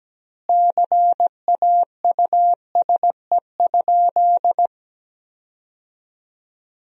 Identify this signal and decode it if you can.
Morse code, heard as a keyed tone, about 17 wpm, 700 Hz. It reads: CAUSE?
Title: 2x Word-Spacing - Sentences from Top 200 Words - 17wpm